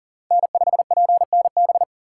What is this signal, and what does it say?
Morse code, heard as a keyed tone, about 49 wpm, 700 Hz. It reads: N5PNB
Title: Speed Racing - USA 1x3 Call Signs 40wpm